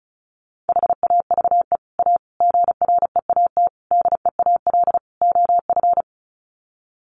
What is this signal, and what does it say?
Morse code, heard as a keyed tone, about 34 wpm, 700 Hz. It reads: HAVE A GREAT DEAL OF
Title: Sets of 5 Words Encore2 35wpm